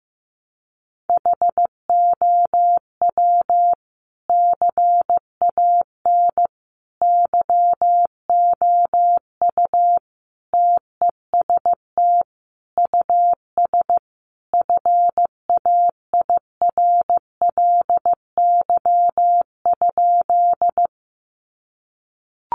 Morse code, heard as a keyed tone, about 15 wpm, 700 Hz. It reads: HOW CAN YOU TEST US FAIRLY?
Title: Sentences from Top 1100 Words 15wpm